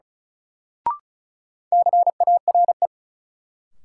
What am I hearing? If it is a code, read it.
Morse code, heard as a keyed tone, about 36 wpm, 700 Hz. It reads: CARE
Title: Sets of 3 Words - New Word Drill - 35wpm